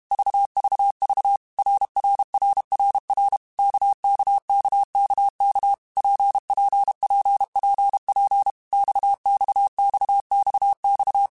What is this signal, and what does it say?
Morse code, heard as a keyed tone, about 32 wpm, 800 Hz. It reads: VVV RRRRR KKKKK PPPPP XXXXX
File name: Basic Letter Groups By7Groups / RKPX / RKPX@30